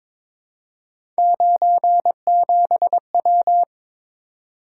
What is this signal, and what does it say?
Morse code, heard as a keyed tone, about 22 wpm, 700 Hz. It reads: 97W
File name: practice / 22wpm / Three Letter-Number Combinations - 22wpm